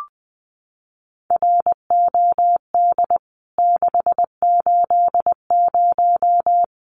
Morse code, heard as a keyed tone, about 20 wpm, 700 Hz. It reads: ROD 680
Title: CWT Exchange 20wpm